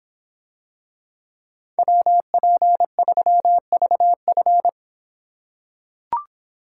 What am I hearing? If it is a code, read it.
Morse code, heard as a keyed tone, about 26 wpm, 700 Hz. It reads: WP3VF